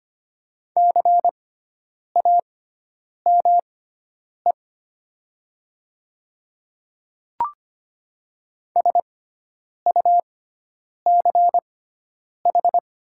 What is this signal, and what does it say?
Morse code, heard as a keyed tone, about 25 wpm, spaced as for 9 wpm, 700 Hz. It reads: CAME SUCH
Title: 6x Character Spacing - Top 200 Words - 25wpm